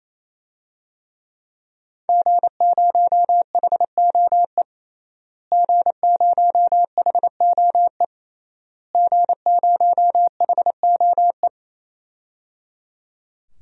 Morse code, heard as a keyed tone, about 28 wpm, 700 Hz. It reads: G0HOE G0HOE G0HOE
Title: DXCC-100 Call Signs - 5 Characters - Repeated 3x - 28wpm